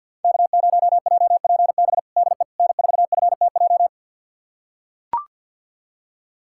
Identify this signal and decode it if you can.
Morse code, heard as a keyed tone, about 46 wpm, 700 Hz. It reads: K0JPB DE N4FTJ
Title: USA Callsign DE Callsign - 5 Characters - 50wpm